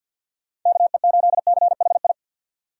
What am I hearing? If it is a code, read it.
Morse code, heard as a keyed tone, about 53 wpm, 700 Hz. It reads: KE8CHI
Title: USA Call Signs - 6 Characters - 3x Character Spacing - 50wpm